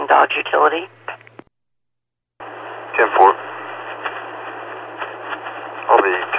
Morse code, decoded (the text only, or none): none